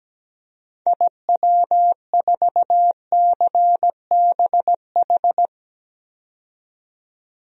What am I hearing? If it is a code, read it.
Morse code, heard as a keyed tone, about 17 wpm, 700 Hz. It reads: IW4CBH
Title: DXCC-100 Call Signs - 6 Characters - 17wpm